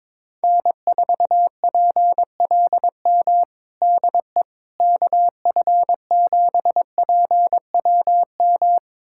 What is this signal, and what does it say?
Morse code, heard as a keyed tone, about 21 wpm, 700 Hz. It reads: N4PLM DE KF7PWM
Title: USA Callsign DE Callsign 22wpm